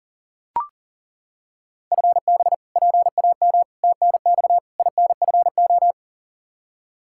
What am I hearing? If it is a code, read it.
Morse code, heard as a keyed tone, about 40 wpm, 700 Hz. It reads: FB PAM TNX INFO
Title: Practice QSO - Exchange 3 - More Info - 40wpm